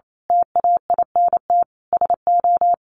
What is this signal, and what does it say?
Morse code, heard as a keyed tone, about 28 wpm, 700 Hz. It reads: TAINT SO
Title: A Sound Of Thunder 28wpm